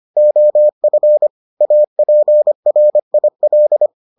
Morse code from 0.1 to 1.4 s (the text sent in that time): OF